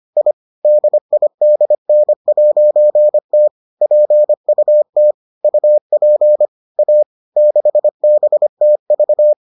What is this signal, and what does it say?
Morse code, heard as a keyed tone, about 25 wpm, 600 Hz. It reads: I DIDN'T PUT UP A 6BTV